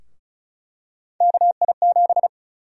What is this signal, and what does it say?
Morse code, heard as a keyed tone, about 36 wpm, 700 Hz. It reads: KI7
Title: USA Call Sign Prefixes - 35wpm